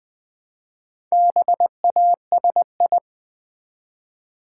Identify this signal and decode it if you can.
Morse code, heard as a keyed tone, about 20 wpm, 700 Hz. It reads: BASI
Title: Build A Word - 1 Letter At A Time - 5-Letter Words - 20wpm